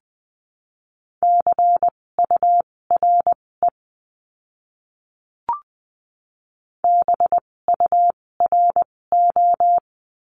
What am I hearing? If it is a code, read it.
Morse code, heard as a keyed tone, about 20 wpm, spaced as for 16 wpm, 700 Hz. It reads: CURE BURO
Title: 256 Lesson 42a - Words 12-20wpm